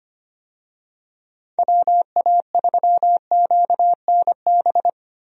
Morse code, heard as a keyed tone, about 25 wpm, 700 Hz. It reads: WA3QNB
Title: USA Call Signs - 6 Characters - Repeated 2x - 25wpm